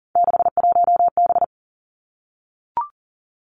Morse code, heard as a keyed tone, about 39 wpm, 700 Hz. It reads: 61B
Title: Three Challenging Letter-Number Combinations 40wpm